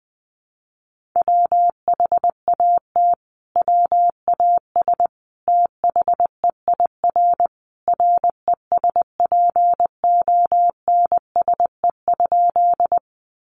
Morse code, heard as a keyed tone, about 20 wpm, 700 Hz. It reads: WHAT WAS THEIR RESPONSE?
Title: Sentences from Top 1400 Words 20wpm